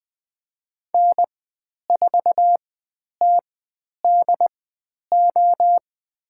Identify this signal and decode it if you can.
Morse code, heard as a keyed tone, about 20 wpm, spaced as for 10 wpm, 700 Hz. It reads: N4TDO
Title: 102 Lesson 17g - Letter u - Call Signs 20wpm